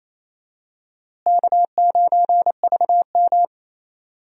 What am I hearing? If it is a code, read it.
Morse code, heard as a keyed tone, about 28 wpm, 700 Hz. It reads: K9VM
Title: USA Call Signs - 4 Characters - Repeated 3x - 28wpm